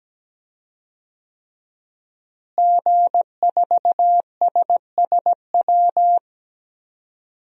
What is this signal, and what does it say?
Morse code, heard as a keyed tone, about 17 wpm, 700 Hz. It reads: G4SSW
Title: DXCC-100 Call Signs - 5 Characters - Repeated 3x - 17wpm